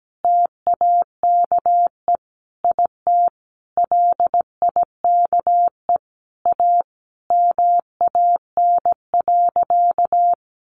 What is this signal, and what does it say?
Morse code, heard as a keyed tone, about 17 wpm, 700 Hz. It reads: TAKE IT LIKE A MAN.
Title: Sentences from Top 300 Words 17wpm